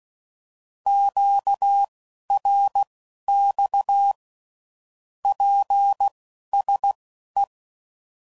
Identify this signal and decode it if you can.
Morse code, heard as a keyed tone, about 16 wpm, spaced as for 11 wpm, 800 Hz. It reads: QRX PSE
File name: CW Code Groups @10WPM / QSOs / QSO kc7haa